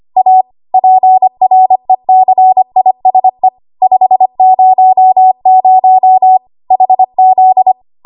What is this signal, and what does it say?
Morse code, heard as a keyed tone, about 25 wpm, 750 Hz. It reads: A PRECISE 500 HZ
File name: Archive / 25 WPM / 25WPM